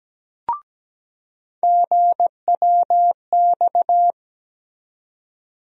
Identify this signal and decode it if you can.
Morse code, heard as a keyed tone, about 17 wpm, 700 Hz. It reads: GWX
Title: USA Call Sign Postfixes - 3 Letters 17wpm